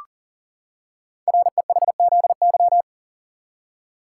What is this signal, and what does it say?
Morse code, heard as a keyed tone, about 40 wpm, 700 Hz. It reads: REHZY
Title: Code Groups - 5 Letters - 40wpm